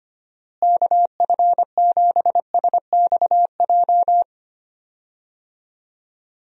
Morse code, heard as a keyed tone, about 25 wpm, 700 Hz. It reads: KF7SXJ
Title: USA Call Signs - 6 Characters - Repeated 2x - 25wpm